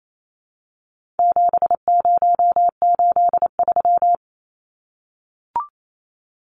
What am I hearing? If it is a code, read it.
Morse code, heard as a keyed tone, about 28 wpm, 700 Hz. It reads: 7083
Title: Numbers - 4-Digit 28wpm